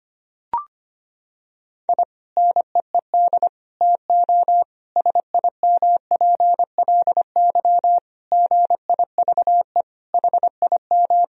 Morse code, heard as a keyed tone, about 25 wpm, 700 Hz. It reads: I NEED TO SIMPLY GIVE HIM
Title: Sentences from Top 1000 Words - Review Part 4 - 25wpm